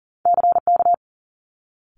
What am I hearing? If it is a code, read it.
Morse code, heard as a keyed tone, about 42 wpm, 700 Hz. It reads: CK
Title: Multi-letter Phonemes Quick Fire 40wpm